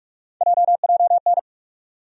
Morse code, heard as a keyed tone, about 46 wpm, 700 Hz. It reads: JJN
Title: USA Call Sign Postfixes - 3 Letters 45wpm